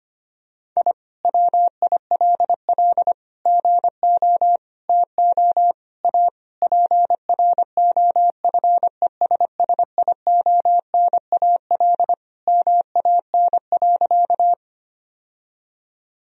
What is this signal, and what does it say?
Morse code, heard as a keyed tone, about 25 wpm, 700 Hz. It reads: I WILL GO TO A PROFESSIONAL MAN.
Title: Sentences from Top 1000 Words - Review Part 4 - 25wpm